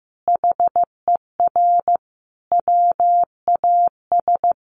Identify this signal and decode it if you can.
Morse code, heard as a keyed tone, about 15 wpm, 700 Hz. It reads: HER WAS
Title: Sentences from Top 1100 Words 15wpm